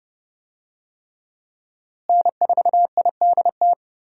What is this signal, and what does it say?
Morse code, heard as a keyed tone, about 31 wpm, 700 Hz. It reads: N4IDT
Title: USA Call Signs - 5 Characters - Repeated 3x - 30wpm